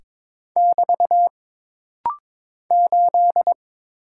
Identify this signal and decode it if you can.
Morse code, heard as a keyed tone, about 22 wpm, 700 Hz. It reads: <BT> 8
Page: This rapid-fire timing is geared to push your ICR (Instant Character Recognition) skills.